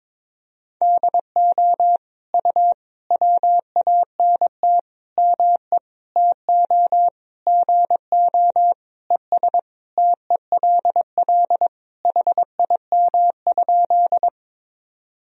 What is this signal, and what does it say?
Morse code, heard as a keyed tone, about 22 wpm, 700 Hz. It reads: DO U WANT ME TO GO ES TELL HIM?
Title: Sentences from Top 200 Words and Abbreviations 22wpm